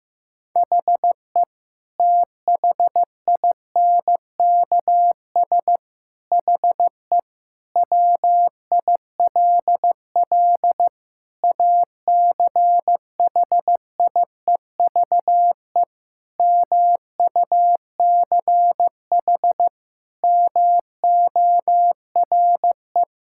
Morse code, heard as a keyed tone, about 16 wpm, 700 Hz. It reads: HE THINKS HE WILL ACHIEVE MUCH MORE
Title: Sentences from Top 1000 Words - Review Part 6 - 15wpm